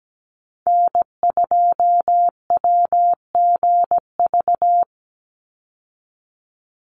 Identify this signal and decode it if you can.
Morse code, heard as a keyed tone, about 17 wpm, 700 Hz. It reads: N2WGV
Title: USA Call Signs 17wpm